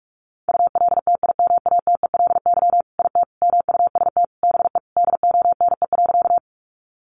Morse code, heard as a keyed tone, about 44 wpm, 700 Hz. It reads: ULTIMATELY IT MUST BE DONE.
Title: Sentences from Top 1200 Words 45wpm